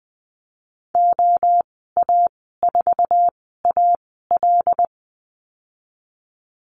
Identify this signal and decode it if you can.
Morse code, heard as a keyed tone, about 20 wpm, spaced as for 15 wpm, 700 Hz. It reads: OA4AL
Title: 179 Lesson 29g - Slash - Call Signs 20wpm